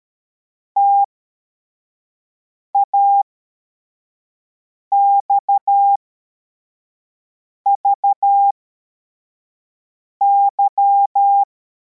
Morse code, heard as a keyed tone, about 13 wpm, spaced as for 4 wpm, 800 Hz. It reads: TAXVY
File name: formation / cw / Text53